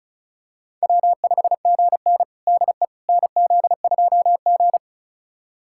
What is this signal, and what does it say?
Morse code, heard as a keyed tone, about 34 wpm, 700 Hz. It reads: W5GN DE NZ2G
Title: USA Callsign DE Callsign - 4 Characters - 35wpm